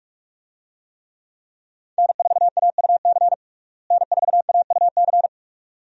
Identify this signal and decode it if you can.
Morse code, heard as a keyed tone, about 46 wpm, 700 Hz. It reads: N4AUC N4AUC
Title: USA Call Signs - 5 Characters - Repeated 2x - 45wpm